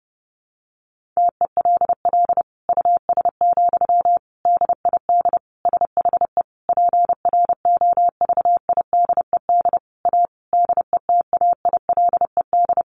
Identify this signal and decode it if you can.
Morse code, heard as a keyed tone, about 30 wpm, 700 Hz. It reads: TELL US, DID SHE PROVIDED A DETAILED
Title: Sentences from Top 1500 Words 30wpm